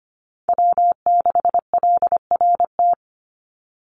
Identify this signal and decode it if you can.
Morse code, heard as a keyed tone, about 25 wpm, 700 Hz. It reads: W6LRT
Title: USA Call Signs 25wpm